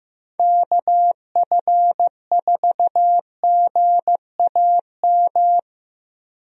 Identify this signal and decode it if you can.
Morse code, heard as a keyed tone, about 15 wpm, 700 Hz. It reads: KF4GAM
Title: USA Call Signs - 6 Characters - 15wpm